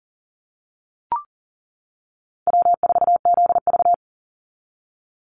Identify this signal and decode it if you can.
Morse code, heard as a keyed tone, about 40 wpm, 700 Hz. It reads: W4ZV
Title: NA SOTA Chaser Callsigns 40wpm